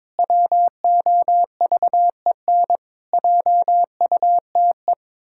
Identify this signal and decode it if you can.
Morse code, heard as a keyed tone, about 22 wpm, 700 Hz. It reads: WOVEN JUTE
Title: Pangram Challenge v2 - 22wpm